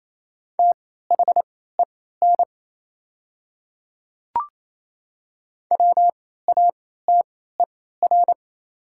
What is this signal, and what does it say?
Morse code, heard as a keyed tone, about 28 wpm, spaced as for 16 wpm, 690 Hz. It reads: THEN WATER
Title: 3x Character Spacing - Top 100 Words - 28wpm